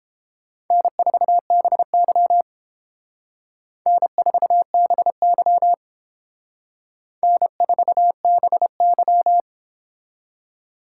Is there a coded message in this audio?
Yes: Morse code from 0.7 to 9.4 s, heard as a keyed tone, about 31 wpm, 700 Hz. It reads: N4BY N4BY N4BY